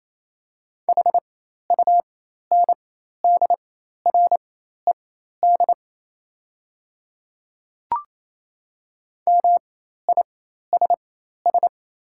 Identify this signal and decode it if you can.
Morse code, heard as a keyed tone, about 27 wpm, spaced as for 13 wpm, 690 Hz. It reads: HUNDRED MISS
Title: 4x Character Spacing - Top 500 Words - 28wpm